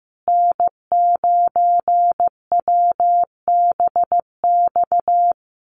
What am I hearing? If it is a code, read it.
Morse code, heard as a keyed tone, about 15 wpm, 700 Hz. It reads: N9WBX
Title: USA Call Signs Repeated 2x 15wpm